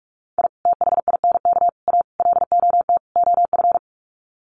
Morse code, heard as a keyed tone, about 44 wpm, 700 Hz. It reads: I THINK A LOT OF
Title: Sets of 5 Words 45wpm